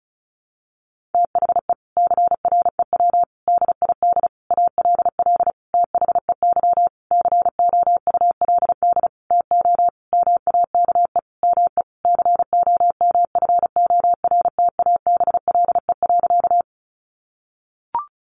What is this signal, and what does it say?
Morse code, heard as a keyed tone, about 34 wpm, 700 Hz. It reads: THE CREW DID ALL THEY COULD TO MAKE ME COMFORTABLE.
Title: Sentences from Top 1500 Words 35wpm